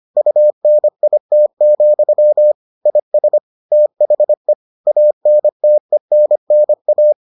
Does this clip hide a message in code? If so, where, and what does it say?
Yes: Morse code from 0.2 to 7.1 s, heard as a keyed tone, about 25 wpm, 600 Hz. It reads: UNIT, IS THE ANTENNA